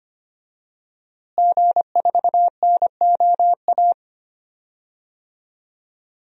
Morse code, heard as a keyed tone, about 25 wpm, 700 Hz. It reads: G4NOA
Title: DXCC-100 Call Signs - 5 Characters - Repeated 3x - 25wpm